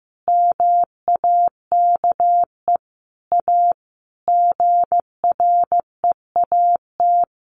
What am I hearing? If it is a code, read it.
Morse code, heard as a keyed tone, about 14 wpm, 700 Hz. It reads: MAKE A GREAT
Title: Sentences from Top 800 Words 15wpm